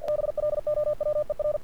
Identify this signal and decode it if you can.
Morse code, heard as a keyed tone, about 47 wpm, 590 Hz. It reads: BLOWER
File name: archived / CRSnet